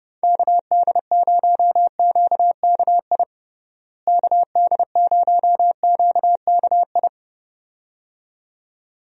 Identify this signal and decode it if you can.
Morse code, heard as a keyed tone, about 30 wpm, 700 Hz. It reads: KD0QKI KD0QKI
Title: USA Call Signs - 6 Characters - Repeated 2x - 30wpm